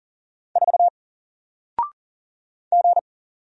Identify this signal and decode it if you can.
Morse code, heard as a keyed tone, about 40 wpm, 700 Hz. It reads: 4 G
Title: Single Character (Letter:Number:Punctuation:Prosign) 40wpm